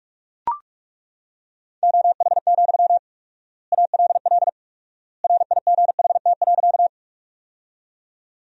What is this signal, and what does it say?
Morse code, heard as a keyed tone, about 42 wpm, 700 Hz. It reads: OH, ALL RIGHT.
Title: 4x Word-Spacing - Sentences from Top 500 Words - 45wpm